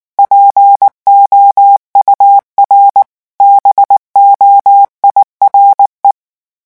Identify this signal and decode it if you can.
Morse code, heard as a keyed tone, about 19 wpm, 800 Hz. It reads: POUR BOIRE
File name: morse mp3 / revision9 citations humoristiques vitesse 18 mots minute